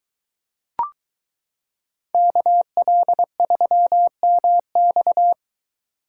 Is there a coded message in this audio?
Yes: Morse code from 2.1 to 5.3 s, heard as a keyed tone, about 23 wpm, 700 Hz. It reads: KL3MX